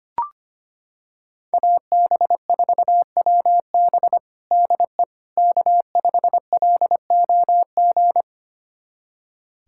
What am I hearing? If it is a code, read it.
Morse code, heard as a keyed tone, about 25 wpm, 700 Hz. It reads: AB4WB DE K5LOG
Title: USA Callsign DE Callsign - 5 Characters - 25wpm